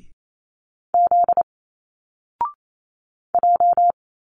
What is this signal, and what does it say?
Morse code, heard as a keyed tone, about 28 wpm, 700 Hz. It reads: Z J
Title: Single Letters 28wpm